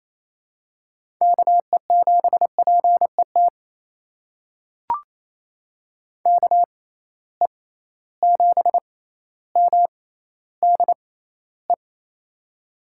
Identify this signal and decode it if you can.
Morse code, heard as a keyed tone, about 28 wpm, spaced as for 14 wpm, 700 Hz. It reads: KE7PET KE7MDE
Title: USA Call Signs - 6 Characters - 6x Character Spacing - 28wpm